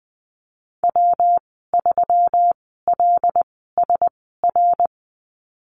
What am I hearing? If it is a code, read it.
Morse code, heard as a keyed tone, about 20 wpm, spaced as for 15 wpm, 700 Hz. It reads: W3LSR
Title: 154 Lesson 25g - Letter p - Call Signs 20wpm